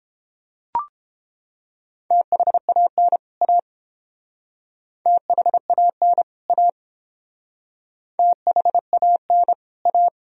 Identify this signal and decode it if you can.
Morse code, heard as a keyed tone, about 29 wpm, 700 Hz. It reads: THAN A THAN A THAN A